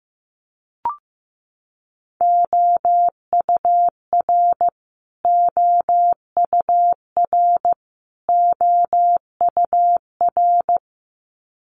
Morse code, heard as a keyed tone, about 14 wpm, 700 Hz. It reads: OUR OUR OUR
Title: Top 500 Words Repeated 15wpm